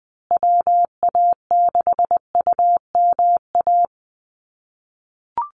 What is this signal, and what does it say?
Morse code, heard as a keyed tone, about 20 wpm, 700 Hz. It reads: WA6UMA
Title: USA Call Signs Repeated 3x 20wpm